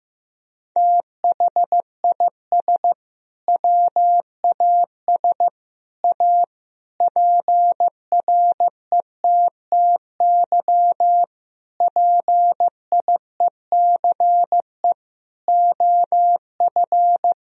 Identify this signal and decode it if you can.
Morse code, heard as a keyed tone, about 16 wpm, 700 Hz. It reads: THIS WAS A PRETTY PIECE OF
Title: Sentences from Top 1000 Words - Review Part 2 - 15wpm